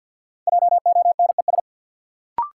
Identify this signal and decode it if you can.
Morse code, heard as a keyed tone, about 46 wpm, 700 Hz. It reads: JONES
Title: Sets of 5 Words - New Word Drill - 50wpm